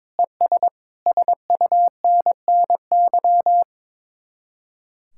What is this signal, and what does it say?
Morse code, heard as a keyed tone, about 22 wpm, 700 Hz. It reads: ES SUNNY
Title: Speed-Racing Common QSO Phrases - 22wpm